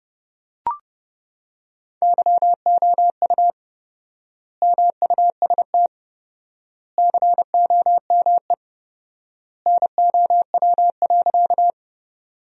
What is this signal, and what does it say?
Morse code, heard as a keyed tone, about 31 wpm, 700 Hz. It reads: YOU MUST COME NOW.
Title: 4x Word-Spacing - Sentences from Top 200 Words - 30wpm